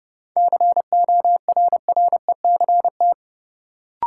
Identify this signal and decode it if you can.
Morse code, heard as a keyed tone, about 31 wpm, 700 Hz. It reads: CORRECT
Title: Sentences from Top 700 Words - New Word Drill - 30wpm